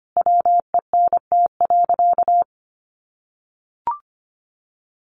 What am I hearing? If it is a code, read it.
Morse code, heard as a keyed tone, about 25 wpm, 700 Hz. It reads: WENT.
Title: Sentences from Top 600 Words 25wpm